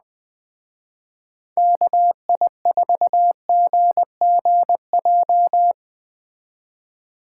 Morse code, heard as a keyed tone, about 19 wpm, 700 Hz. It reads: KI4GGJ
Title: Speed Racing - USA Call Signs 17wpm